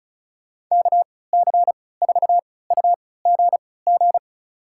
Morse code, heard as a keyed tone, about 34 wpm, spaced as for 20 wpm, 700 Hz. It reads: KC4UGG
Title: USA Call Signs - 6 Characters - 3x Character Spacing - 35wpm